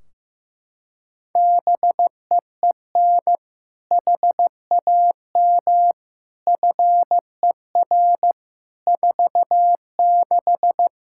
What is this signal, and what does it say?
Morse code, heard as a keyed tone, about 15 wpm, 700 Hz. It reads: BEEN HAM FER 46
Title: K1USN SST QSO Practice 15wpm